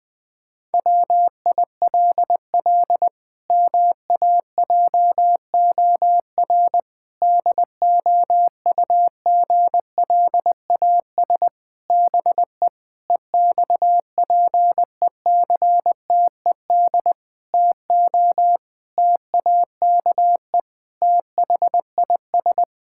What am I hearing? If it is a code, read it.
Morse code, heard as a keyed tone, about 21 wpm, 700 Hz. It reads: WILL MAJOR DOUGLAS BE EXPECTED TO TAKE THIS
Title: Pangram Challenge v2 - 20wpm